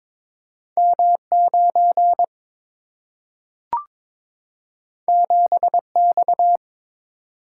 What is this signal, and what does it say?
Morse code, heard as a keyed tone, about 22 wpm, 700 Hz. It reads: M9 7X
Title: Single-Letter Single-Number 22wpm